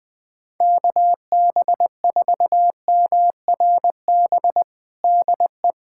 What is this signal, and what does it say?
Morse code, heard as a keyed tone, about 20 wpm, 700 Hz. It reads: KB4MRB DE
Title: USA Callsign DE Callsign 20wpm